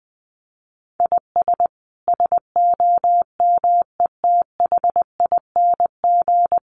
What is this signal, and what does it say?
Morse code, heard as a keyed tone, about 20 wpm, 700 Hz. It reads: IS SOMETHING
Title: Sentences from Top 700 Words 20wpm